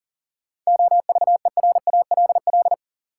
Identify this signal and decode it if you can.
Morse code, heard as a keyed tone, about 39 wpm, 700 Hz. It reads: OVERALL
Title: Sentences from Top 1000 Words - Review Part 7 - New Word Drill - 40wpm